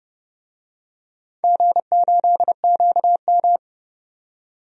Morse code, heard as a keyed tone, about 31 wpm, 700 Hz. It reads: G8QM
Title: DXCC-100 Call Signs - 4 Characters - 30wpm